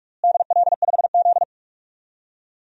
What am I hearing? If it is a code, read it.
Morse code, heard as a keyed tone, about 46 wpm, 700 Hz. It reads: DL5Z
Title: DXCC-100 Call Signs - 4 Characters - Repeated 2x - 45wpm